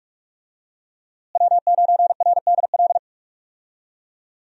Morse code, heard as a keyed tone, about 47 wpm, 700 Hz. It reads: W9RDL
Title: USA Call Signs - 5 Characters - Repeated 2x - 45wpm